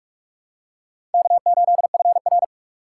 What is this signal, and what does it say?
Morse code, heard as a keyed tone, about 42 wpm, 700 Hz. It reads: K8FR